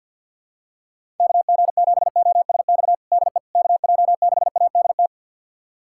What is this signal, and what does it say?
Morse code, heard as a keyed tone, about 49 wpm, 700 Hz. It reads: KG7OSX DE KJ6ADT
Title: USA Callsign DE Callsign 50wpm